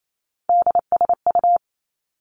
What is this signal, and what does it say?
Morse code, heard as a keyed tone, about 28 wpm, 700 Hz. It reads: DSU
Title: Three Challenging Letter-Number Combinations 28wpm